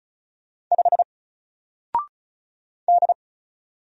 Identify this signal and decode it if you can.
Morse code, heard as a keyed tone, about 35 wpm, 700 Hz. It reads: 5 D
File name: Single-Letter-Number-35wpm